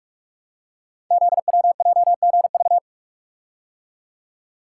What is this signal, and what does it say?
Morse code, heard as a keyed tone, about 44 wpm, 690 Hz. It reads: ZWJGV
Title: Code Groups - 5 Letters - 45wpm